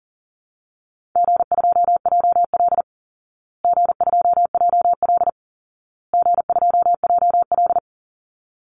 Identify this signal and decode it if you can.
Morse code, heard as a keyed tone, about 40 wpm, 700 Hz. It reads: G2JL G2JL G2JL
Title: Easy Call Signs Repeated 3x - 40wpm